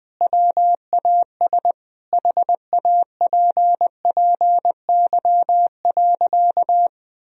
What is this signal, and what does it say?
Morse code, heard as a keyed tone, about 21 wpm, 700 Hz. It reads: WAS HAPPY.
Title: Sentences from Top 1000 Words - Review Part 1 - 20wpm